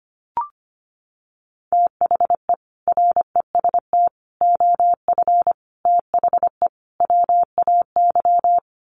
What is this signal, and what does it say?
Morse code, heard as a keyed tone, about 25 wpm, 700 Hz. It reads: THE REST OF THE WAY
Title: Sets of 5 Words Encore 25wpm